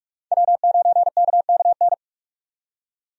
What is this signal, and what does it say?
Morse code, heard as a keyed tone, about 46 wpm, 700 Hz. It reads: W9KKN
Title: USA Call Signs - 5 Characters - 45wpm